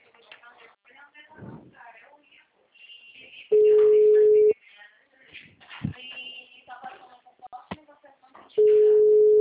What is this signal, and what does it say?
Ringback tone (Europe, 425 Hz), with speech.